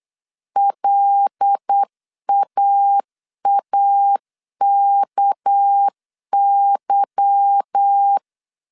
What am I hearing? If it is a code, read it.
Morse code, heard as a keyed tone, about 8 wpm, 800 Hz. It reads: LAAKY
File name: CW / DET / 1 9 1500C S A 8wpm 5637